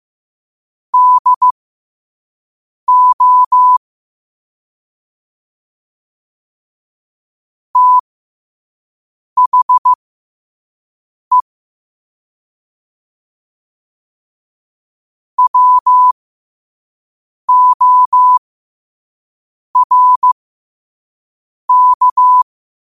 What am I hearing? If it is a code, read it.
Morse code, heard as a keyed tone, about 15 wpm, spaced as for 5 wpm, 1000 Hz. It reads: DO THE WORK
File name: qotdfiles / 5wpm